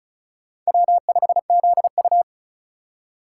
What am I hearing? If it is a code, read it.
Morse code, heard as a keyed tone, about 44 wpm, 700 Hz. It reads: W5ZU